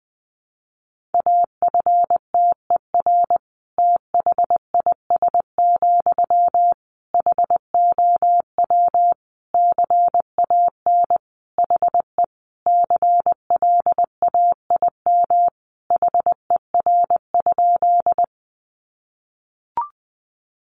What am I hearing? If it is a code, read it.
Morse code, heard as a keyed tone, about 20 wpm, 700 Hz. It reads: AFTER THIS, HOW CAN HE CLAIM HER?
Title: Sentences from Top 1500 Words 20wpm